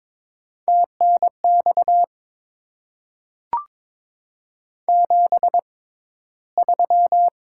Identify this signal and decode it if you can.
Morse code, heard as a keyed tone, about 22 wpm, spaced as for 13 wpm, 700 Hz. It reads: TNX 73
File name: Top 20 QSO Elements - 6x Character Spacing - 22wpm